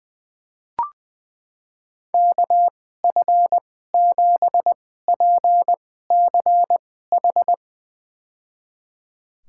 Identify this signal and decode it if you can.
Morse code, heard as a keyed tone, about 20 wpm, spaced as for 14 wpm, 700 Hz. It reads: KF7PCH
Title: USA Call Signs - 2x Character Spacing - 20wpm